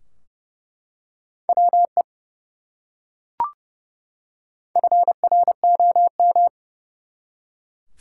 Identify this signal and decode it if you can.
Morse code, heard as a keyed tone, about 30 wpm, 700 Hz. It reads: WE FROM
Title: Top 180 QSO Elements 30wpm